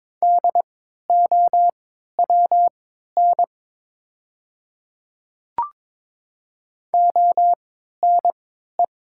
Morse code, heard as a keyed tone, about 22 wpm, spaced as for 12 wpm, 700 Hz. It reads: DOWN ONE
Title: 3x Character Spacing - Top 100 Words - 22wpm